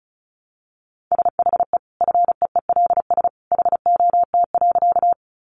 Morse code, heard as a keyed tone, about 34 wpm, 700 Hz. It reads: SHE FEELS HOT.
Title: Additional Easy Phrases 35wpm